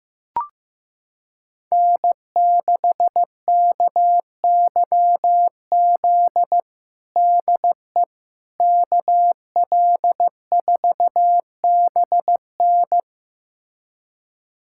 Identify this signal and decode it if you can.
Morse code, heard as a keyed tone, about 15 wpm, 700 Hz. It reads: N6KYZ DE KL4BN
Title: USA Callsign DE Callsign - 5 Characters - 15wpm